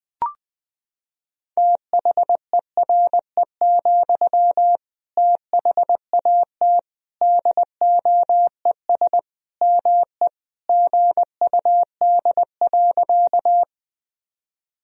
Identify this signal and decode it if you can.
Morse code, heard as a keyed tone, about 20 wpm, 700 Hz. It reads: THERE, THAT DOES ME GUD.
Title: Sentences from Top 200 Words and Abbreviations 20wpm